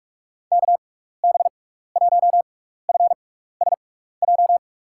Morse code, heard as a keyed tone, about 46 wpm, spaced as for 16 wpm, 700 Hz. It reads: KB1FSJ
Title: USA Call Signs - 6 Characters - 6x Character Spacing - 45wpm